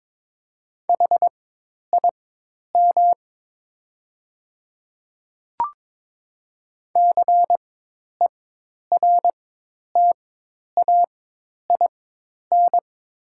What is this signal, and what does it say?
Morse code, heard as a keyed tone, about 22 wpm, spaced as for 10 wpm, 700 Hz. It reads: HIM CERTAIN
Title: 4x Character Spacing - Top 500 Words - 22wpm